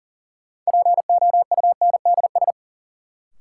Morse code, heard as a keyed tone, about 37 wpm, 700 Hz. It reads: POUNDS
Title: Sentences from Top 1000 Words - Review Part 2 - New Word Drill - 40wpm